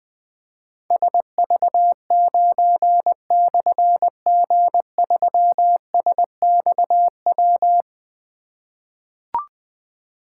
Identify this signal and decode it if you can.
Morse code, heard as a keyed tone, about 20 wpm, 700 Hz. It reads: SV9/G3SXW
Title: Difficult Call Signs 20wpm